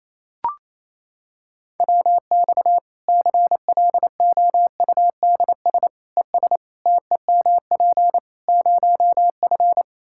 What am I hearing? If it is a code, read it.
Morse code, heard as a keyed tone, about 28 wpm, 700 Hz. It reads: WX CLOUDS ES TEMP 0F
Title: Practice QSO - Exchange 2 - Info - 28wpm